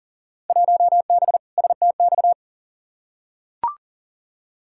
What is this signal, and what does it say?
Morse code, heard as a keyed tone, about 40 wpm, 700 Hz. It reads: 1B STX
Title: Field Day Exchanges - 40wpm